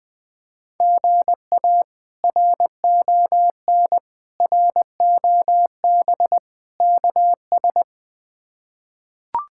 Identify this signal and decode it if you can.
Morse code, heard as a keyed tone, about 21 wpm, 700 Hz. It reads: GA RON ROB KS
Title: K1USN SST QSO Practice - 20wpm